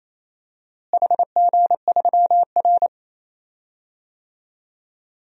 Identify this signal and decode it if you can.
Morse code, heard as a keyed tone, about 28 wpm, 700 Hz. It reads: HG3R
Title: DXCC-100 Call Signs - 4 Characters - Repeated 2x - 28wpm